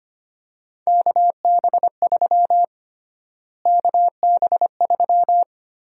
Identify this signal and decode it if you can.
Morse code, heard as a keyed tone, about 25 wpm, 700 Hz. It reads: KB3 KB3
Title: USA Call Sign Prefixes - Repeated 2x - 25wpm